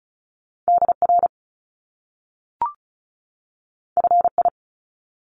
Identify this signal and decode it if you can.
Morse code, heard as a keyed tone, about 36 wpm, 700 Hz. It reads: DR FI
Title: Top Two Letter Combinations in Words 35wpm